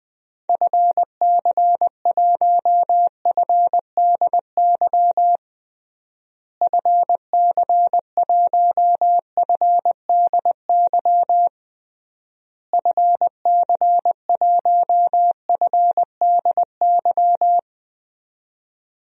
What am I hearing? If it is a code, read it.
Morse code, heard as a keyed tone, about 20 wpm, 700 Hz. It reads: FC1FDY FC1FDY FC1FDY
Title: DXCC-100 Call Signs - 6 Characters - Repeated 3x - 20wpm